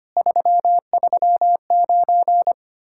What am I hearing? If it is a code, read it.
Morse code, heard as a keyed tone, about 33 wpm, 700 Hz. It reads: 339